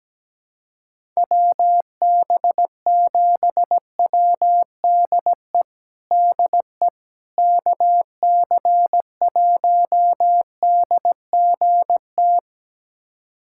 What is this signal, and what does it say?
Morse code, heard as a keyed tone, about 17 wpm, 700 Hz. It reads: WB7WDE DE KC1DGT
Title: USA Callsign DE Callsign - 6 Characters - 17wpm